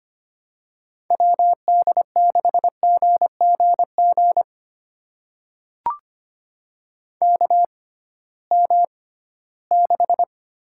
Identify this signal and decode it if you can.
Morse code, heard as a keyed tone, about 25 wpm, spaced as for 16 wpm, 700 Hz. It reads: WD6GGG KM6
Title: USA Call Signs - 6 Characters - 6x Character Spacing - 25wpm